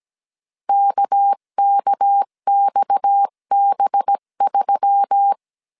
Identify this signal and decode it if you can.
Morse code, heard as a keyed tone, about 17 wpm, 800 Hz. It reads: KKXB3
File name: CW / DET / 1 9 1500C S B 16wpm 2856